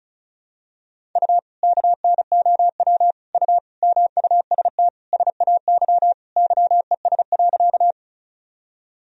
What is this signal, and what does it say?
Morse code, heard as a keyed tone, about 36 wpm, 700 Hz. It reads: U KNOW U MUST SAY YES.
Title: Sentences from Top 500 Words and Abbreviations 35wpm